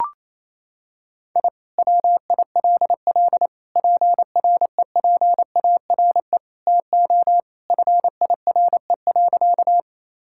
Morse code, heard as a keyed tone, about 28 wpm, 700 Hz. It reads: I WILL PREPARE TO FIRE.
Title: Sentences from Top 1000 Words - Review Part 6 - 28wpm